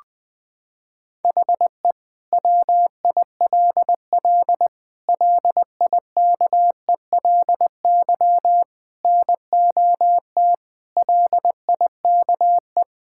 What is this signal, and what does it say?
Morse code, heard as a keyed tone, about 20 wpm, 700 Hz. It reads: HE WILL LIKELY NOT LIKE
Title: Sentences from Top 1000 Words - Review Part 5 - 20wpm